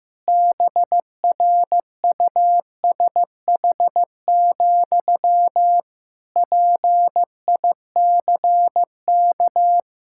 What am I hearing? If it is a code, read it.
Morse code, heard as a keyed tone, about 15 wpm, 700 Hz. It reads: BRUSH, PICK
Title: Pangram Challenge v2 - 15wpm